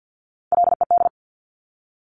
Morse code, heard as a keyed tone, about 50 wpm, 700 Hz. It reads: LED
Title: Top Three Letter Combinations in Words 50wpm